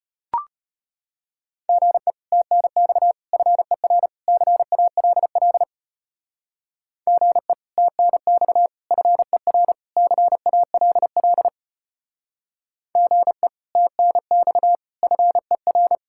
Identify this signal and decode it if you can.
Morse code, heard as a keyed tone, about 31 wpm, 700 Hz. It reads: GE TNX FER CALL GE TNX FER CALL GE TNX FER
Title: Speed-Racing Common QSO Phrases - 25wpm